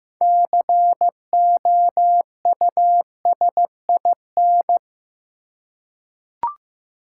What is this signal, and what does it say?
Morse code, heard as a keyed tone, about 15 wpm, 700 Hz. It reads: COUSIN
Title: Six Letter Words - 15wpm